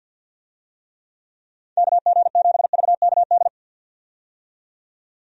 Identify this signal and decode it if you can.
Morse code, heard as a keyed tone, about 49 wpm, 700 Hz. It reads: KG7VKD
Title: USA Call Signs - 6 Characters - 50wpm